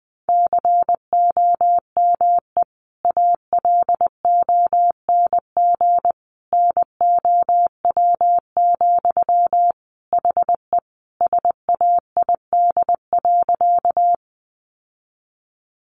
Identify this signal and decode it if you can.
Morse code, heard as a keyed tone, about 20 wpm, 700 Hz. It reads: COME ALONG NOW, HE SAID.
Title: Sentences from Top 700 Words 20wpm